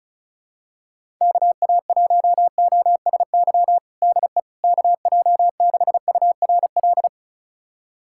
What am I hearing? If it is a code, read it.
Morse code, heard as a keyed tone, about 36 wpm, 700 Hz. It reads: KA1OSY DE KJ6URL
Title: USA Callsign DE Callsign - 6 Characters - 35wpm